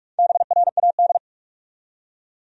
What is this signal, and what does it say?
Morse code, heard as a keyed tone, about 46 wpm, 700 Hz. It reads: BRAD
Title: USA Common Names - Max 4 Letters - 45wpm